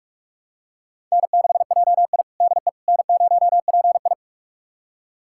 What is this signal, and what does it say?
Morse code, heard as a keyed tone, about 44 wpm, 690 Hz. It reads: N6JI DE N0PI
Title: USA Callsign DE Callsign - 4 Characters - 45wpm